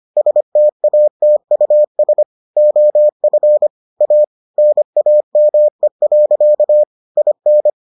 Morse code, heard as a keyed tone, about 25 wpm, 600 Hz. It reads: STATUS OF A NAME. IN